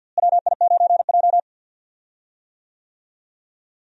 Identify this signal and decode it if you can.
Morse code, heard as a keyed tone, about 52 wpm, 700 Hz. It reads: WI9J
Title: USA Call Signs - 4 Characters - Repeated 3x - 50wpm